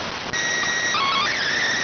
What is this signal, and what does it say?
SSTV (slow-scan television), PD 290.